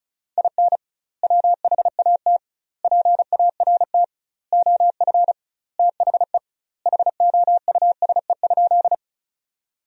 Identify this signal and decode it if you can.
Morse code, heard as a keyed tone, about 34 wpm, 700 Hz. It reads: IN WHAT PART OF THE HOUSE?
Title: 2x Word-Spacing - Sentences from Top 200 Words - 35wpm